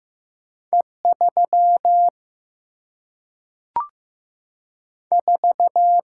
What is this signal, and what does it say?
Morse code, heard as a keyed tone, about 15 wpm, 700 Hz. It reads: E 3 4
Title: Single-Letter Single-Number 15wpm